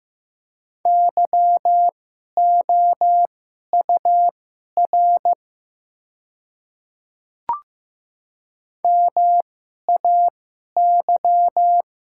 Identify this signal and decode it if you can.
Morse code, heard as a keyed tone, about 15 wpm, spaced as for 11 wpm, 700 Hz. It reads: YOUR MAY
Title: 2x Character Spacing - Top 200 Words - 15wpm